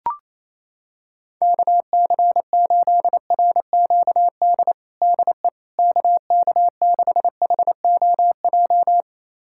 Morse code, heard as a keyed tone, about 28 wpm, 700 Hz. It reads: KC8RQD DE KK6HOJ
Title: USA Callsign DE Callsign - 6 Characters - 28wpm